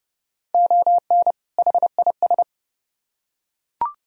Morse code, heard as a keyed tone, about 29 wpm, 700 Hz. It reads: ON HIS